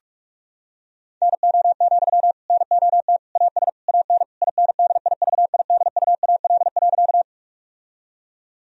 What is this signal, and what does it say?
Morse code, heard as a keyed tone, about 41 wpm, 700 Hz. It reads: NO, NOT AS AN INDIVIDUAL.
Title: Sentences from Top 1000 Words - Review Part 4 - 45wpm